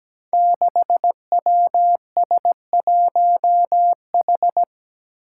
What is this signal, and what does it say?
Morse code, heard as a keyed tone, about 17 wpm, 700 Hz. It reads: 6WS1H
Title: Five Challenging Letter-Number Combinations 17wpm